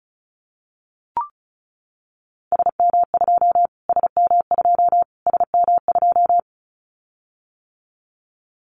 Morse code, heard as a keyed tone, about 34 wpm, 700 Hz. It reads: SM2 SM2 SM2
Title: Callsign Prefixes for Top 25 DXCC - Repeated 3x - 35wpm